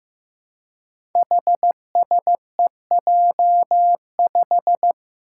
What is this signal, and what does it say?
Morse code, heard as a keyed tone, about 15 wpm, 700 Hz. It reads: HSEJ5
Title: Five Challenging Letter-Number Combinations 15wpm